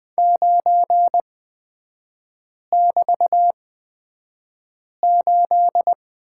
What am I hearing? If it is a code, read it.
Morse code, heard as a keyed tone, about 20 wpm, spaced as for 5 wpm, 700 Hz. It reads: 9<BT>8
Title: 243 Lesson 39e - Letter z - Three Characters 20wpm